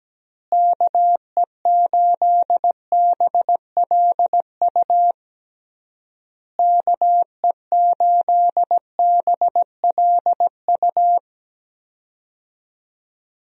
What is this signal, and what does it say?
Morse code, heard as a keyed tone, about 17 wpm, 700 Hz. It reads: KE8BLU KE8BLU
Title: USA Call Signs - 6 Characters - Repeated 3x - 17wpm